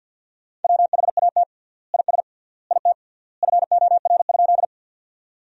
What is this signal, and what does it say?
Morse code, heard as a keyed tone, about 50 wpm, 700 Hz. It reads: WHAT IS IT FOR?
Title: 3x Word-Spacing - Sentences from Top 100 Words - 50wpm